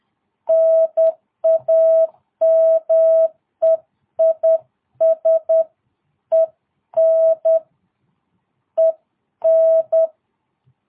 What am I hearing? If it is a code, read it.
Morse code, heard as a keyed tone, about 10 wpm, 650 Hz. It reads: NAME IS EN EN
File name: jh8cbh / 2022cw